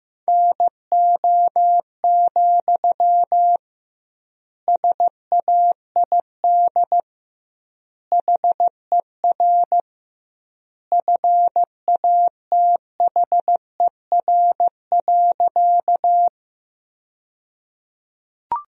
Morse code, heard as a keyed tone, about 16 wpm, 700 Hz. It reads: NO, SAID HER FATHER.
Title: 2x Word-Spacing - Sentences from Top 300 Words - 15wpm